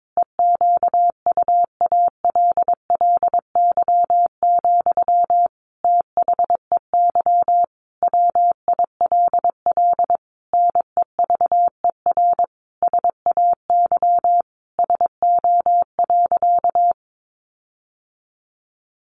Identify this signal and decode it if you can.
Morse code, heard as a keyed tone, about 22 wpm, 700 Hz. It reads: EQUALLY, THEY WILL NEVER SAY SO.
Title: Sentences from Top 1400 Words 22wpm